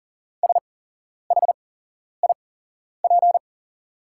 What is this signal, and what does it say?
Morse code, heard as a keyed tone, about 38 wpm, spaced as for 11 wpm, 690 Hz. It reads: SHIP
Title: 8x Character Spacing - Top 500 Words - 40wpm